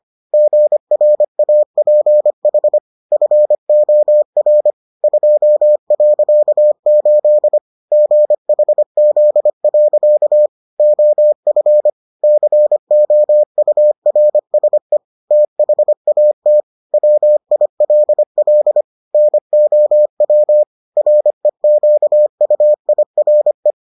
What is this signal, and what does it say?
Morse code, heard as a keyed tone, about 25 wpm, 600 Hz. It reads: GRAPH FOR 2.8 GHZ. OF COURSE THAT WILL NOW REQUIRE